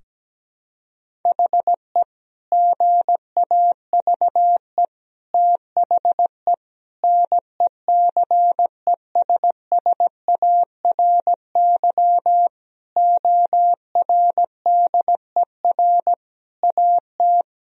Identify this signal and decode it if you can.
Morse code, heard as a keyed tone, about 17 wpm, 700 Hz. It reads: HE GAVE THE NECESSARY ORDER AT
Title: Sentences from Top 1000 Words - Review Part 2 - 17wpm